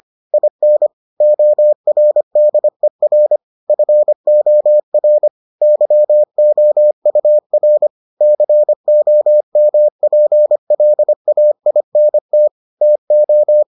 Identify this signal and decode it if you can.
Morse code, heard as a keyed tone, about 25 wpm, 600 Hz. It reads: IN ORDER FOR YOUR COMPLAINT TO